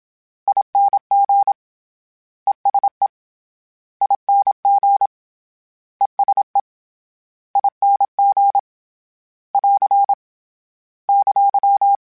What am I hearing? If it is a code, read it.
Morse code, heard as a keyed tone, about 26 wpm, 800 Hz. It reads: ING ESE ING ESE ING <AR> !
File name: Short Runs / Short Runs-mp3s / ESE ING @25